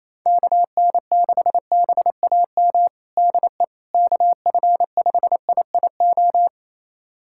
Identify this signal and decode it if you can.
Morse code, heard as a keyed tone, about 27 wpm, 700 Hz. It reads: KN6BAM DE KF5IIO
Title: USA Callsign DE Callsign - 6 Characters - 28wpm